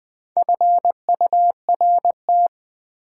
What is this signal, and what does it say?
Morse code, heard as a keyed tone, about 21 wpm, 690 Hz. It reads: FURT
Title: Build A Word - 4 Letters At A Time - 11-Letter Words - 20wpm